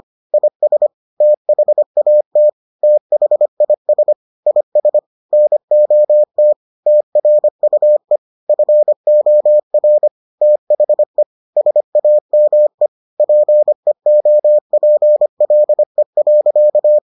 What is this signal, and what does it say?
Morse code, heard as a keyed tone, about 25 wpm, 600 Hz. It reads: IS THAT THIS IS NOT TRUE FOR THE SAME PEOPLE.